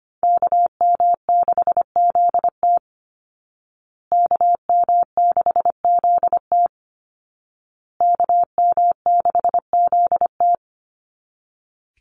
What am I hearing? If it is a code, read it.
Morse code, heard as a keyed tone, about 25 wpm, 700 Hz. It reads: KM6ZT KM6ZT KM6ZT
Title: Call Signs Repeated 2x - 25wpm